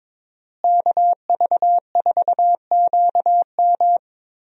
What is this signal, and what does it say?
Morse code, heard as a keyed tone, about 22 wpm, 700 Hz. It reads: KV4QM
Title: USA Call Signs - 5 Characters - Repeated 2x - 22wpm